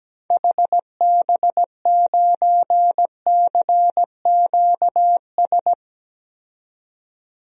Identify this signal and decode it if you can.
Morse code, heard as a keyed tone, about 17 wpm, 700 Hz. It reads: HB9CQS
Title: DXCC-100 Call Signs - 6 Characters - 17wpm